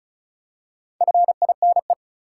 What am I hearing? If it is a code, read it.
Morse code, heard as a keyed tone, about 36 wpm, 700 Hz. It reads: FINE
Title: Four Letter Words - 35wpm